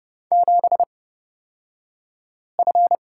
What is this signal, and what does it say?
Morse code, heard as a keyed tone, about 30 wpm, spaced as for 5 wpm, 700 Hz. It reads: 7F
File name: Lesson 36d - BT - Two Characters 30wpm